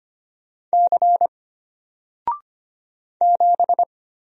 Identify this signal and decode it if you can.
Morse code, heard as a keyed tone, about 25 wpm, 700 Hz. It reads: C 7
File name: Single-Letter-Number-25wpm